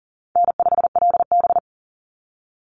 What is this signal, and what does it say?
Morse code, heard as a keyed tone, about 40 wpm, 700 Hz. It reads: N5LB
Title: Easy Call Signs - 40wpm